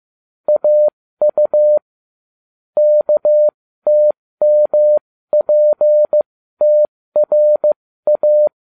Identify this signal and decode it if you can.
Morse code, heard as a keyed tone, about 15 wpm, spaced as for 13 wpm, 600 Hz. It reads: AU KTMPTRA